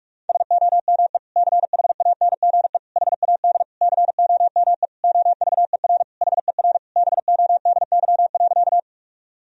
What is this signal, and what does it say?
Morse code, heard as a keyed tone, about 44 wpm, 700 Hz. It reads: SOME CHANGE HAD COME OVER HER BODY.
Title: Sentences from Top 1000 Words - Review Part 2 - 45wpm